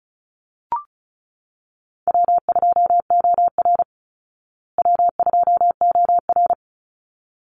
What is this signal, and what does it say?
Morse code, heard as a keyed tone, about 34 wpm, 700 Hz. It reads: W2OR W2OR
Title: Easy Call Signs Repeated 2x - 35wpm